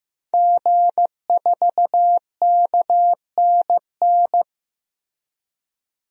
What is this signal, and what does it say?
Morse code, heard as a keyed tone, about 15 wpm, 700 Hz. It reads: G4KNN
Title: DXCC-100 Call Signs - 5 Characters - 15wpm